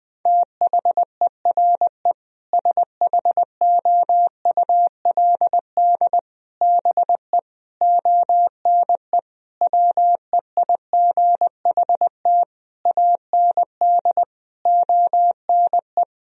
Morse code, heard as a keyed tone, about 20 wpm, 700 Hz. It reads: THERE SHOULD BE ONE WEIGHT AND ONE
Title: Sentences from Top 1000 Words - Review Part 3 - 20wpm